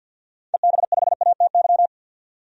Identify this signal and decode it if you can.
Morse code, heard as a keyed tone, about 49 wpm, 700 Hz. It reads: EB5ATY
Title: DXCC-100 Call Signs - 6 Characters - 50wpm